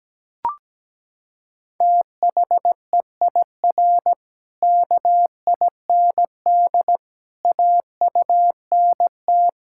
Morse code, heard as a keyed tone, about 17 wpm, 700 Hz. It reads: THEIR KIND AUNT
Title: Pangram Challenge v2 - 17wpm